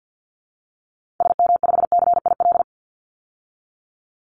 Morse code, heard as a keyed tone, about 50 wpm, 700 Hz. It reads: SM5CIL
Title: Call Signs - 50wpm